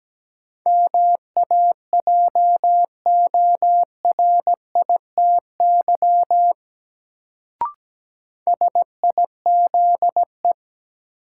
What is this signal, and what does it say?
Morse code, heard as a keyed tone, about 17 wpm, 700 Hz. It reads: MAJORITY SIZE
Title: Sets of 3 Words - New Word Drill - 17wpm